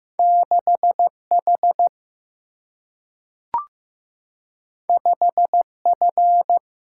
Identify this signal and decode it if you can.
Morse code, heard as a keyed tone, about 15 wpm, 700 Hz. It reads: 6H 5F
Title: Single-Letter Single-Number 15wpm